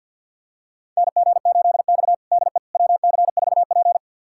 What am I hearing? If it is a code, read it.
Morse code, heard as a keyed tone, about 47 wpm, 700 Hz. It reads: NG8X DE WC4P
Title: USA Callsign DE Callsign - 4 Characters - 50wpm